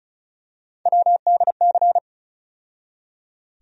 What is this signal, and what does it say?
Morse code, heard as a keyed tone, about 36 wpm, 700 Hz. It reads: WDC
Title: USA Call Sign Postfixes - 3 Letters 35wpm